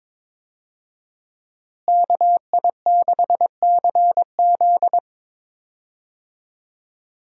Morse code, heard as a keyed tone, about 22 wpm, 700 Hz. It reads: KI6CZ
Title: USA Call Signs - 5 Characters - Repeated 2x - 22wpm